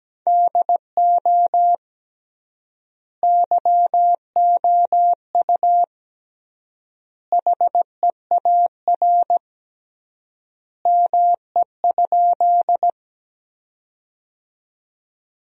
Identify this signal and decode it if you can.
Morse code, heard as a keyed tone, about 17 wpm, 700 Hz. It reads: DO YOU HEAR ME?
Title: 3x Word-Spacing - Sentences from Top 300 Words - 17wpm